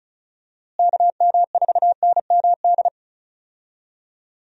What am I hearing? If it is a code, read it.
Morse code, heard as a keyed tone, about 36 wpm, 700 Hz. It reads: KM4NMD
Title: USA Call Signs - 6 Characters - Repeated 2x - 35wpm